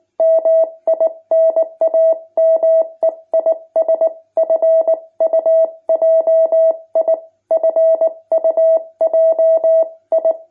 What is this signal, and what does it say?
Morse code, heard as a keyed tone, about 19 wpm, 650 Hz. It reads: MI NAME IS FUJI FUJI
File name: jh8cbh / 2021cw